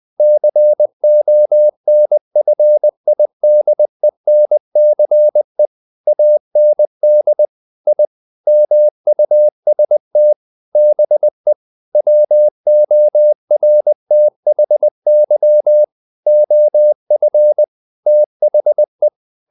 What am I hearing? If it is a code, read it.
Morse code, heard as a keyed tone, about 20 wpm, 600 Hz. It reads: CONFIDENCE AND I MUST BE WORTHY OF THE
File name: the old man and the sea / the old man and the sea part 1 20wpm